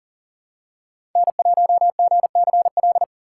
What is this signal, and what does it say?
Morse code, heard as a keyed tone, about 39 wpm, 700 Hz. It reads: N1GCL
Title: USA Call Signs - 2x Character Spacing - 40wpm